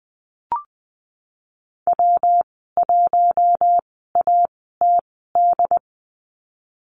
Morse code, heard as a keyed tone, about 20 wpm, spaced as for 15 wpm, 700 Hz. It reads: W1ATD
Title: 167 Lesson 27g - Number 7 - Call Signs 20wpm